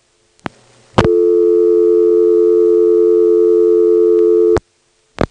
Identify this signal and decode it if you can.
Dial tone (North America), 350 + 440 Hz.